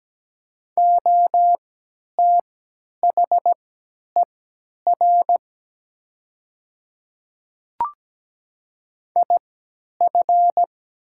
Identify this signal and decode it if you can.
Morse code, heard as a keyed tone, about 17 wpm, spaced as for 10 wpm, 700 Hz. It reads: OTHER IF
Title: 3x Character Spacing - Top 300 Words - 17wpm